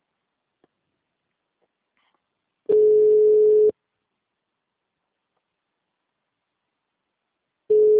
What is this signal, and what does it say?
Ringback tone (Europe, 425 Hz).
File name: out- -2002